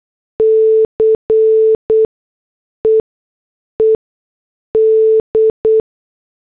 Morse code, heard as a keyed tone, about 8 wpm, spaced as for 6 wpm, 440 Hz. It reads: CEED